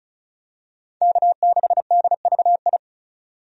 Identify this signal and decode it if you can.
Morse code, heard as a keyed tone, about 34 wpm, 700 Hz. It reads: K6DVI
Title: USA Call Signs - 5 Characters - 35wpm